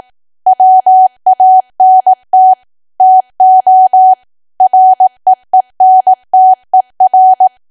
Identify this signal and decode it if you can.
Morse code, heard as a keyed tone, about 18 wpm, 750 Hz. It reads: WANT TO REENTER
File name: cwa / W1AW-Level2 / 18WPM